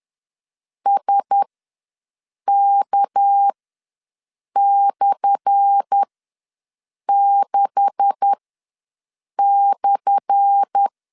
Morse code, heard as a keyed tone, about 10 wpm, spaced as for 6 wpm, 800 Hz. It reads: SK/6/
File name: LEZIONI / 10 SIGNS / 10A SLASH ws 10wpm 4121